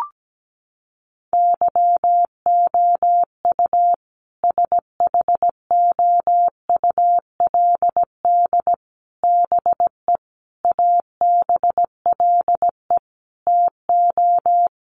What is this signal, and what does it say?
Morse code, heard as a keyed tone, about 17 wpm, 700 Hz. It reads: YOU SHOULD BE ABLE TO
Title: Sets of 5 Words Encore 17wpm